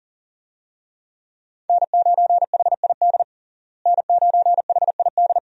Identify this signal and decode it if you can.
Morse code, heard as a keyed tone, about 40 wpm, 700 Hz. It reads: N9HID N9HID
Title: USA Call Signs - 5 Characters - Repeated 2x - 40wpm